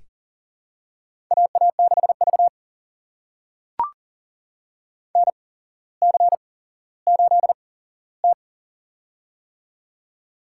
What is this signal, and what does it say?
Morse code, heard as a keyed tone, about 40 wpm, spaced as for 17 wpm, 700 Hz. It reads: AA6V NC8T
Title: USA Call Signs - 4 Characters - 8x Character Spacing - 40wpm